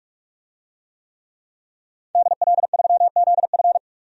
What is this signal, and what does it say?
Morse code, heard as a keyed tone, about 46 wpm, 700 Hz. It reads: DL3ZF
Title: DXCC-100 Call Signs - 5 Characters - 45wpm